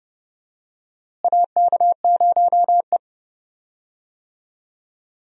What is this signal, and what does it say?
Morse code, heard as a keyed tone, about 30 wpm, 700 Hz. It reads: AK0E
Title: USA Call Signs - 4 Characters - Repeated 3x - 30wpm